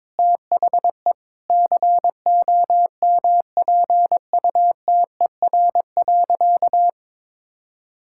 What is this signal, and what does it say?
Morse code, heard as a keyed tone, about 22 wpm, 700 Hz. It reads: THE COMPUTER.
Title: Sentences from Top 1000 Words - Review Part 6 - 22wpm